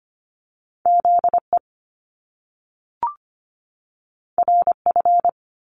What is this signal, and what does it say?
Morse code, heard as a keyed tone, about 25 wpm, 700 Hz. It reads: ZE RF
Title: Top Two Letter Combinations in Words 25wpm